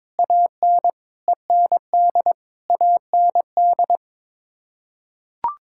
Morse code, heard as a keyed tone, about 22 wpm, 700 Hz. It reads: AN END AND
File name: Sets of 9 Words - 22wpm